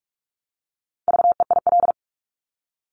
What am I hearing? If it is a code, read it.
Morse code, heard as a keyed tone, about 46 wpm, 700 Hz. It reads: VEIL
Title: Multi-letter Phonemes Repeated 3x 45wpm